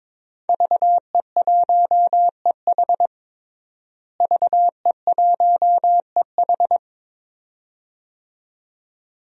Morse code, heard as a keyed tone, about 22 wpm, 700 Hz. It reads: VE1EH VE1EH
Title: DXCC-100 Call Signs - 5 Characters - Repeated 2x - 22wpm